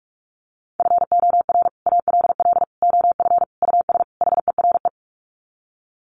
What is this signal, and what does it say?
Morse code, heard as a keyed tone, about 44 wpm, 700 Hz. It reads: FOR ALL OF US HERE
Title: Sets of 5 Words 45wpm